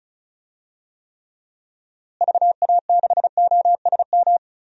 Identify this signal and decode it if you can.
Morse code, heard as a keyed tone, about 34 wpm, 700 Hz. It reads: VA6OSM
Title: DXCC-100 Call Signs - 6 Characters - Repeated 2x - 35wpm